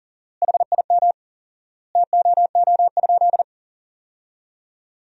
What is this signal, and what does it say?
Morse code, heard as a keyed tone, about 40 wpm, 700 Hz. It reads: HIM TOO?
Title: 4x Word-Spacing - Sentences from Top 500 Words - 40wpm